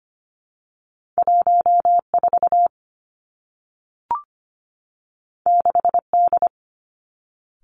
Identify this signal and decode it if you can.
Morse code, heard as a keyed tone, about 25 wpm, 700 Hz. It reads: 1 4 6 D
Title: Two Challenging Letter-Number Combinations 25wpm